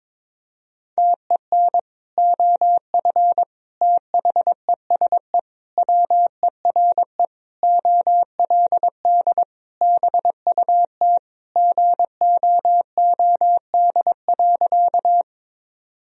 Morse code, heard as a keyed tone, about 22 wpm, 700 Hz. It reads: TEN OF THESE WERE OLD BUT GOOD.
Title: Sentences from Top 1000 Words - Review Part 2 - 22wpm